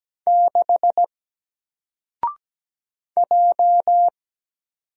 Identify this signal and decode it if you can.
Morse code, heard as a keyed tone, about 17 wpm, 700 Hz. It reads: 6 J
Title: Single Character (Letter:Number:Punctuation:Prosign) 17wpm